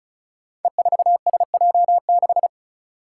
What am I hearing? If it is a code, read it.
Morse code, heard as a keyed tone, about 36 wpm, 700 Hz. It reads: E4SJ6
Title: Five Challenging Letter-Number Combinations 35wpm